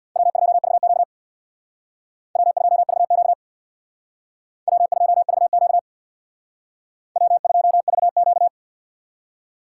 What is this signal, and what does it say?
Morse code, heard as a keyed tone, about 61 wpm, 700 Hz. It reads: W2VQ W2VQ W2VQ W2VQ
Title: Speed Racing - USA Call Signs 50wpm